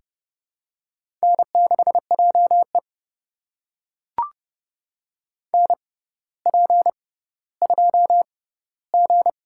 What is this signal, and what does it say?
Morse code, heard as a keyed tone, about 32 wpm, spaced as for 16 wpm, 700 Hz. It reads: N6JE NP2G
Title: USA Call Signs - 4 Characters - 6x Character Spacing - 30wpm